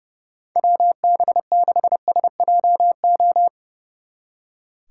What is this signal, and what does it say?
Morse code, heard as a keyed tone, about 36 wpm, 700 Hz. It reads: WB6SJO
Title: Speed Racing - USA Call Signs 30wpm